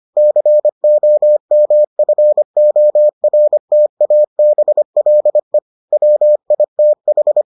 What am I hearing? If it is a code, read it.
Morse code, heard as a keyed tone, about 25 wpm, 600 Hz. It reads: COMFORTABLE WITH